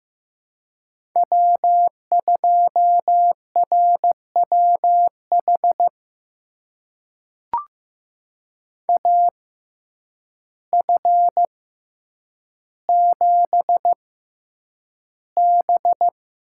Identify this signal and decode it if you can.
Morse code, heard as a keyed tone, about 15 wpm, spaced as for 8 wpm, 700 Hz. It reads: W2RWH AF7B
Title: USA Call Signs - 6x Character Spacing - 15wpm